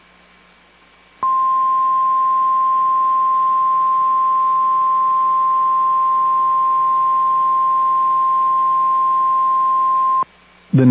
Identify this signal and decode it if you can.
Emergency Alert System (SAME) tones (the 1050 Hz attention tone). It is heard from 1.2 to 10.3 s.